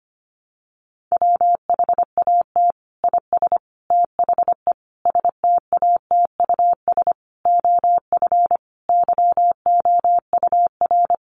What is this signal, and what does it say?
Morse code, heard as a keyed tone, about 25 wpm, 700 Hz. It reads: WHAT IS THE STATUS OF YOUR
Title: Sentences from Top 1400 Words 25wpm